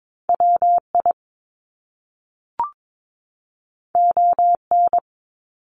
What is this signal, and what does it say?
Morse code, heard as a keyed tone, about 22 wpm, 700 Hz. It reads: WI ON
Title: Top Two Letter Combinations in Words 22wpm